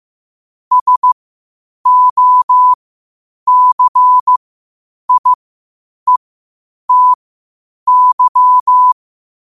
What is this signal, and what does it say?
Morse code, heard as a keyed tone, about 15 wpm, spaced as for 8 wpm, 1000 Hz. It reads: SOCIETY
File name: qotdfiles / 8wpm